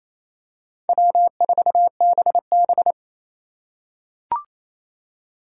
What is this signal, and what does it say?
Morse code, heard as a keyed tone, about 28 wpm, 700 Hz. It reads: W4BB
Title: USA Call Signs - 4 Characters - 3x Character Spacing - 28wpm